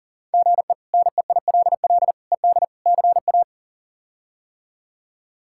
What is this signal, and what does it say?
Morse code, heard as a keyed tone, about 40 wpm, 700 Hz. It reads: GE NEILL ED CA
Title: K1USN SST QSO Practice - 40wpm